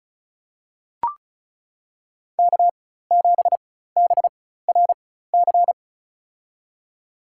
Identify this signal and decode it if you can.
Morse code, heard as a keyed tone, about 34 wpm, spaced as for 16 wpm, 700 Hz. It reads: K7BRC
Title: USA Call Signs - 5 Characters - 4x Character Spacing - 35wpm